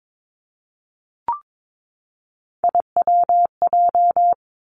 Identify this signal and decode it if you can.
Morse code, heard as a keyed tone, about 22 wpm, 700 Hz. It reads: IWJ
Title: Three Challenging Letter-Number Combinations 22wpm